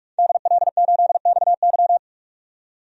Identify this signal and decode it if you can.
Morse code, heard as a keyed tone, about 44 wpm, 690 Hz. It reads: DL8XY
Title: DXCC-100 Call Signs - 5 Characters - 45wpm